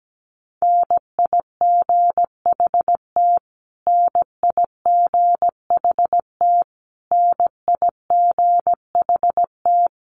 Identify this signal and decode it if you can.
Morse code, heard as a keyed tone, about 17 wpm, 700 Hz. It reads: NIGHT NIGHT NIGHT
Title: Top 500 Words Repeated 17wpm